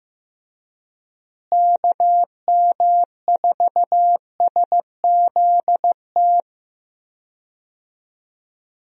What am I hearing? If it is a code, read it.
Morse code, heard as a keyed tone, about 15 wpm, 700 Hz. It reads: KM4SZT
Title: USA Call Signs - 6 Characters - Repeated 3x - 15wpm